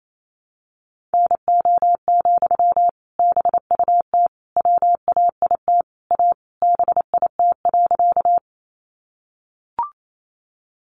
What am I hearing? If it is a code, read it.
Morse code, heard as a keyed tone, about 28 wpm, 700 Hz. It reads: NO, BUT WAIT A BIT.
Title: Sentences from Top 900 Words 28wpm